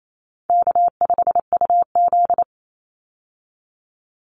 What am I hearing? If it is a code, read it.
Morse code, heard as a keyed tone, about 28 wpm, 700 Hz. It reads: K5UZ
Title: Easy Call Signs - 28wpm